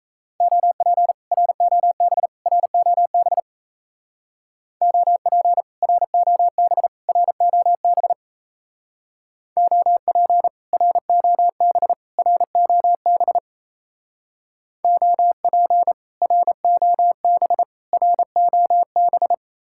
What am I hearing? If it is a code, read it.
Morse code, heard as a keyed tone, about 36 wpm, 700 Hz. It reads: OP ROB ROB OP ROB ROB OP ROB ROB OP ROB ROB